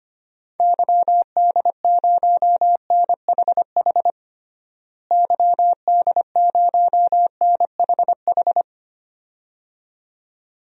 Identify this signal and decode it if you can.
Morse code, heard as a keyed tone, about 25 wpm, 700 Hz. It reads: YD0NHH YD0NHH
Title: DXCC-100 Call Signs - 6 Characters - Repeated 2x - 25wpm